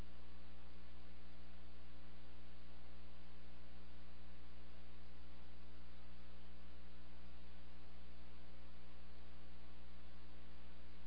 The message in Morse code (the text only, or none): none